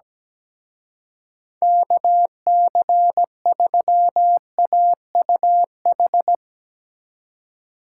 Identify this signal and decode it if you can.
Morse code, heard as a keyed tone, about 17 wpm, 700 Hz. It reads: KC3AUH
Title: Speed Racing - USA 2x3 Call Signs 17wpm